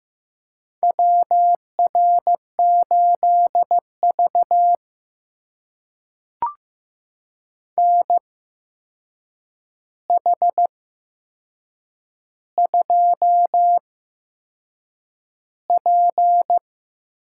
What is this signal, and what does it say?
Morse code, heard as a keyed tone, about 15 wpm, spaced as for 6 wpm, 700 Hz. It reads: WR8V NH2P
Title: USA Call Signs - 4 Characters - 8x Character Spacing - 15wpm